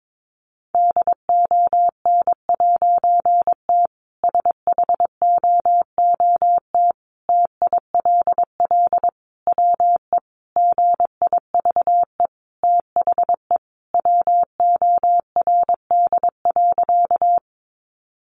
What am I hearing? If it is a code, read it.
Morse code, heard as a keyed tone, about 22 wpm, 700 Hz. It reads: DON'T SHOOT TILL WE GIVE THE WORD.
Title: A Sound Of Thunder 22wpm